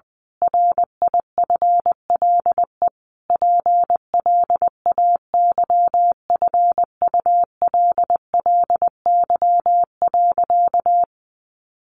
Morse code, heard as a keyed tone, about 20 wpm, 700 Hz. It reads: RIFLE PLAYFULLY.
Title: A Sound Of Thunder 20wpm